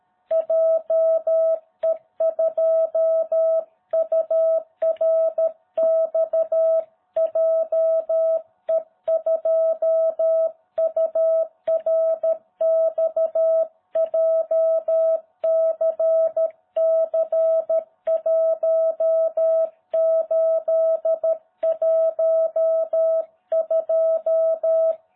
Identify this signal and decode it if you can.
Morse code, heard as a keyed tone, about 13 wpm, 660 Hz. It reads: JE2URX JE2URX JCC1812